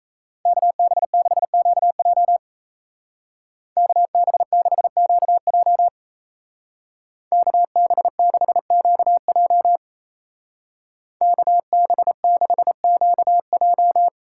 Morse code, heard as a keyed tone, about 35 wpm, 700 Hz. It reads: KB6QJ KB6QJ KB6QJ KB6QJ
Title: Speed Racing - USA 2x2 Call Signs 28wpm